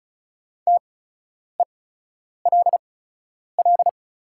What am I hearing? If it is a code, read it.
Morse code, heard as a keyed tone, about 34 wpm, spaced as for 10 wpm, 690 Hz. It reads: TELL
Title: 8x Character Spacing - Top 200 Words - 35wpm